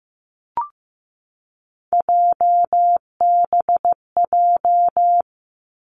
Morse code, heard as a keyed tone, about 15 wpm, 700 Hz. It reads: JBJ
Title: Three Challenging Letter-Number Combinations 15wpm